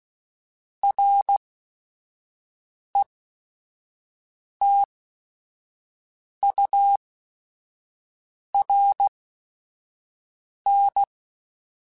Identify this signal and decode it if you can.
Morse code, heard as a keyed tone, about 16 wpm, spaced as for 5 wpm, 800 Hz. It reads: RETURN